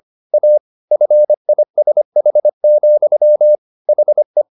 Morse code, heard as a keyed tone, about 25 wpm, 600 Hz. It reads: A FISH, HE
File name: the old man and the sea part 1 25wpm